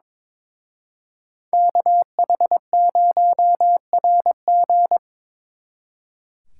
Morse code, heard as a keyed tone, about 26 wpm, 700 Hz. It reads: KH0RG
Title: Speed Racing - USA 2x2 Call Signs 22wpm